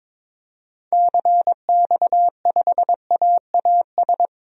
Morse code, heard as a keyed tone, about 22 wpm, 700 Hz. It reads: CX5AAS
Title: DXCC-100 Call Signs - 6 Characters - Repeated 3x - 22wpm